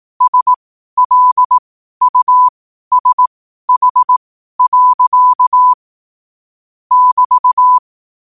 Morse code, heard as a keyed tone, about 18 wpm, spaced as for 12 wpm, 1000 Hz. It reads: SLUSH. <BT>
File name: qotdfiles / 12wpm